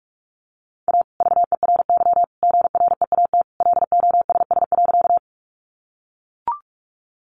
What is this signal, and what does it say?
Morse code, heard as a keyed tone, about 44 wpm, 700 Hz. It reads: A VERY GREAT LOSS.
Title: Sentences from Top 1400 Words 45wpm